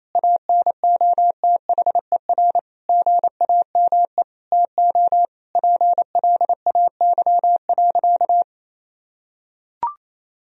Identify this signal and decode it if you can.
Morse code, heard as a keyed tone, about 28 wpm, 700 Hz. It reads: ANOTHER GAME TO PLAY.
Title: Sentences from Top 1000 Words - Review Part 3 - 28wpm